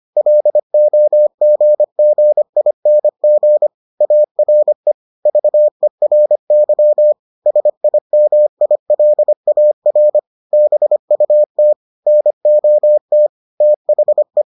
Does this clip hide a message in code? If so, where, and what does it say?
Yes: Morse code from 0.2 to 14.4 s, heard as a keyed tone, about 25 wpm, 600 Hz. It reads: LOGGING ARE VERY SIMILAR BUT NOT THE